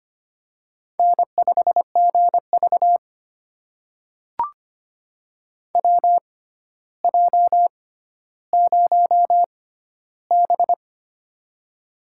Morse code, heard as a keyed tone, about 25 wpm, spaced as for 13 wpm, 700 Hz. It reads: N5GV WJ0B
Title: USA Call Signs - 4 Characters - 6x Character Spacing - 25wpm